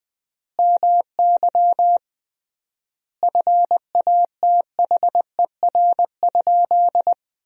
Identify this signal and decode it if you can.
Morse code, heard as a keyed tone, about 20 wpm, 700 Hz. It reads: MY FATHER?
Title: 3x Word-Spacing - Sentences from Top 400 Words - 20wpm